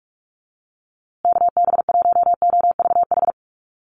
Morse code, heard as a keyed tone, about 44 wpm, 700 Hz. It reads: KB1OVH
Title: USA Call Signs Repeated 2x 45wpm